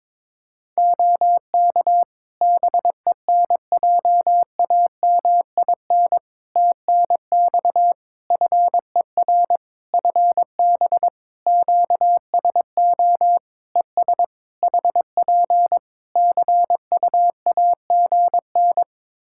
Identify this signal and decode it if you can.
Morse code, heard as a keyed tone, about 22 wpm, 700 Hz. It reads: OK BENJAMIN TNX FER FB QSO ES HP CUAGN
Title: Practice QSO - Exchange 4 - Ending - 22wpm